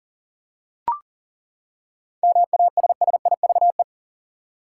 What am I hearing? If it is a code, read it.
Morse code, heard as a keyed tone, about 40 wpm, 700 Hz. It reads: MASSIVE
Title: Seven Letter Words - 40wpm